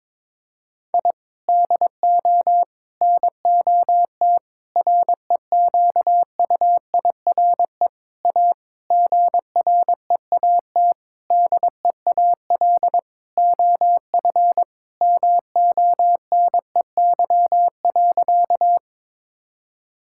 Morse code, heard as a keyed tone, about 22 wpm, 700 Hz. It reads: I DO NOT REQUIRE A GREAT DEAL OF MONEY.
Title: Sentences from Top 1000 Words - Review Part 6 - 22wpm